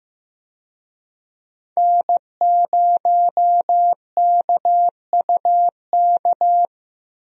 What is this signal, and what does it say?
Morse code, heard as a keyed tone, about 15 wpm, 700 Hz. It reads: N0KUK
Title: USA Call Signs - 5 Characters - Repeated 2x - 15wpm